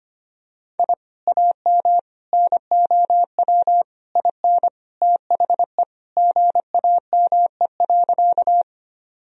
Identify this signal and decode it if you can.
Morse code, heard as a keyed tone, about 25 wpm, 700 Hz. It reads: I AM NOW IN THE GAME.
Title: Sentences from Top 1000 Words - Review Part 3 - 25wpm